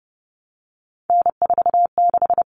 Morse code, heard as a keyed tone, about 28 wpm, 700 Hz. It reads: N46
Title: Common Cut Numbers - 3-Digit 30wpm